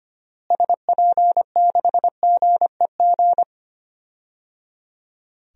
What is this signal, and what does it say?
Morse code, heard as a keyed tone, about 25 wpm, 700 Hz. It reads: SP6GEG
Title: DXCC-100 Call Signs - 6 Characters - Repeated 3x - 25wpm